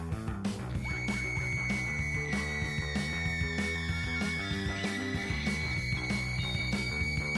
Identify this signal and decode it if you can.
Dial-up modem handshake.